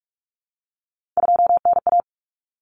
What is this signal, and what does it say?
Morse code, heard as a keyed tone, about 44 wpm, 700 Hz. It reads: 2NA
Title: Common Cut Numbers - 3-Digit 45wpm